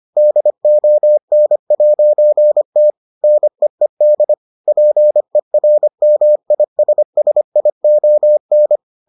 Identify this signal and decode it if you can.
Morse code, heard as a keyed tone, about 25 wpm, 600 Hz. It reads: DON'T NEED PERMISSION